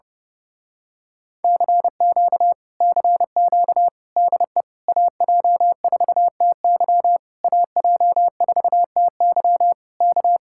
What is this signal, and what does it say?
Morse code, heard as a keyed tone, about 30 wpm, 700 Hz. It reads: CQ CQ DE AJ4TY AJ4TY K
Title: Speed-Racing Common QSO Phrases - 30wpm